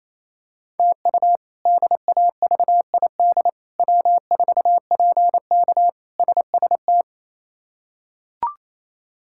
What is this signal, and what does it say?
Morse code, heard as a keyed tone, about 28 wpm, 700 Hz. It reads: TU DAVID W4PK SST
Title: K1USN SST QSO Practice - 28wpm